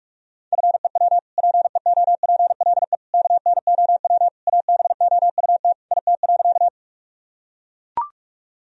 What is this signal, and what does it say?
Morse code, heard as a keyed tone, about 41 wpm, 700 Hz. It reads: FEW PEOPLE KNOW ABOUT IT.
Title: Sentences from Top 1000 Words - Review Part 1 - 45wpm